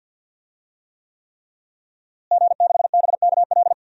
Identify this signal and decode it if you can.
Morse code, heard as a keyed tone, about 46 wpm, 700 Hz. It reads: G6BKL
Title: DXCC-100 Call Signs - 5 Characters - Repeated 3x - 50wpm